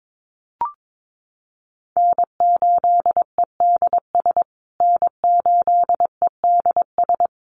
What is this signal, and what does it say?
Morse code, heard as a keyed tone, about 22 wpm, 700 Hz. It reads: N8EDS N8EDS
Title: USA Call Signs Repeated 2x 22wpm